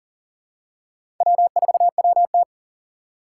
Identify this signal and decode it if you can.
Morse code, heard as a keyed tone, about 37 wpm, 700 Hz. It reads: W4WT
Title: USA Call Signs - 3x Character Spacing - 40wpm